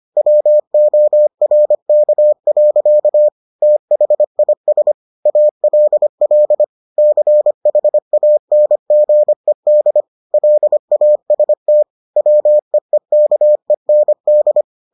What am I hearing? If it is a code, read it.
Morse code, heard as a keyed tone, about 25 wpm, 600 Hz. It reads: WORK. THIS ALL CHANGED LAST WEEKEND